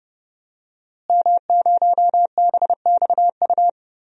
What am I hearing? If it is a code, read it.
Morse code, heard as a keyed tone, about 30 wpm, 700 Hz. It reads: M0BXU
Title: DXCC-100 Call Signs - 5 Characters - Repeated 3x - 30wpm